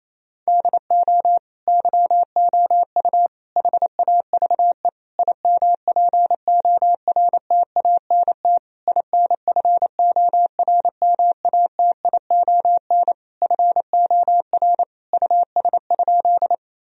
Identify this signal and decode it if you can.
Morse code, heard as a keyed tone, about 28 wpm, 700 Hz. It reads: DO YOU HAVE IMPORTANT INFORMATION FOR US?
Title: Sentences from Top 1000 Words - Review Part 3 - 28wpm